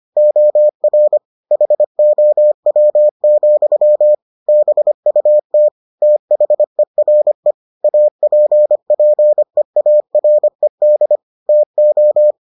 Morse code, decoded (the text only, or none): OR HOW, BUT THERE APPEARED TO